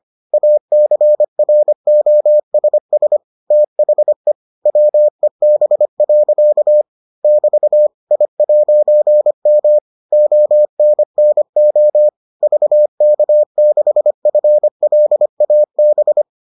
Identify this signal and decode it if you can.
Morse code, heard as a keyed tone, about 25 wpm, 600 Hz. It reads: ACROSS THE WEB. <BT> I'M ONNO VK6FLAB